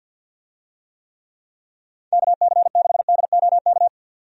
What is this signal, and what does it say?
Morse code, heard as a keyed tone, about 51 wpm, 700 Hz. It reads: KC6DOK
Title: USA Call Signs - 6 Characters - 50wpm